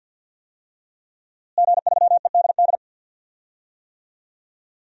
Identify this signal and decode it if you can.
Morse code, heard as a keyed tone, about 48 wpm, 700 Hz. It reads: G3EDD
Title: DXCC-100 Call Signs - 5 Characters - Repeated 3x - 50wpm